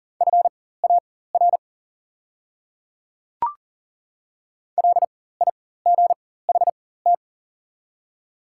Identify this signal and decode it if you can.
Morse code, heard as a keyed tone, about 39 wpm, spaced as for 18 wpm, 700 Hz. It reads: FAR LIGHT
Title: 4x Character Spacing - Top 300 Words - 40wpm